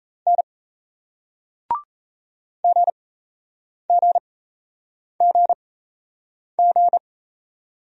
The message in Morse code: N GGGG

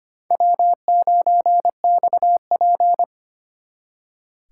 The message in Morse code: W9XP

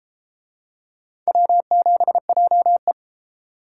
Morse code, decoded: W7JE

